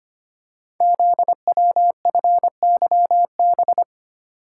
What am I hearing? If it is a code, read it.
Morse code, heard as a keyed tone, about 25 wpm, 700 Hz. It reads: ZWFYB
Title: Code Groups - 5 Letters - 25wpm